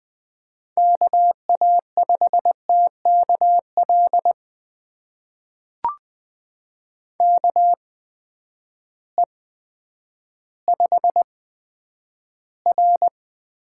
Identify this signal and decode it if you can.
Morse code, heard as a keyed tone, about 20 wpm, spaced as for 10 wpm, 700 Hz. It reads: KA5TKL KE5R
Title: USA Call Signs - 6 Characters - 8x Character Spacing - 20wpm